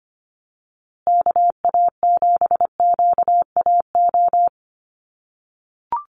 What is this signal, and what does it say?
Morse code, heard as a keyed tone, about 25 wpm, 700 Hz. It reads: KA7QAO
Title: USA Call Signs Repeated 2x 25wpm